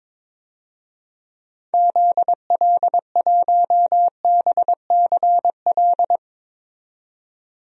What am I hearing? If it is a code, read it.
Morse code, heard as a keyed tone, about 22 wpm, 700 Hz. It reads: ZL1BCL
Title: DXCC-100 Call Signs - 6 Characters - 22wpm